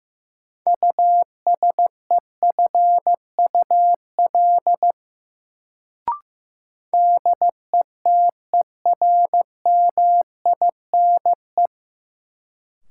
Morse code, heard as a keyed tone, about 15 wpm, 700 Hz. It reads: USEFUL DETERMINE
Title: Sentences from Top 700 Words - New Word Drill - 15wpm